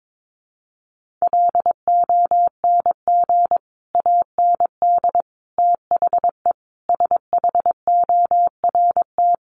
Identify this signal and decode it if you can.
Morse code, heard as a keyed tone, about 22 wpm, 700 Hz. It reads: LONG AND THE SHORT
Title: Sentences from Top 400 Words 22wpm